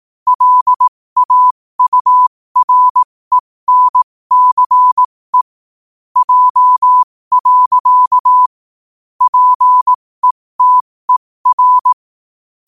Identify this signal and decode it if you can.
Morse code, heard as a keyed tone, about 18 wpm, spaced as for 15 wpm, 1000 Hz. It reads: LAURENCE J. PETER
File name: qotdfiles / 15wpm